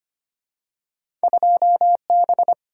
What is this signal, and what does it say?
Morse code, heard as a keyed tone, about 25 wpm, 700 Hz. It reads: 2B
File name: Two Letter-Number Combinations - 25wpm